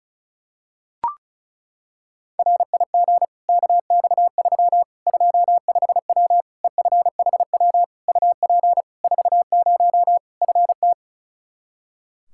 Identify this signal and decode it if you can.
Morse code, heard as a keyed tone, about 36 wpm, 700 Hz. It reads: RIG KX3 25W EFHW UP 40 FT
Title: Practice QSO - Exchange 2 - Info - 35wpm